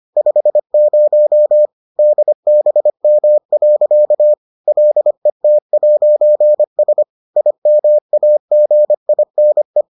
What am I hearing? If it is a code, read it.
Morse code, heard as a keyed tone, about 25 wpm, 600 Hz. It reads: 50 DBM. LET'S IMAGINE